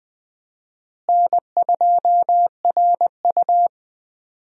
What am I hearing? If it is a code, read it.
Morse code, heard as a keyed tone, about 21 wpm, 700 Hz. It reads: N2RU
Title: USA Call Signs - 4 Characters - Repeated 3x - 20wpm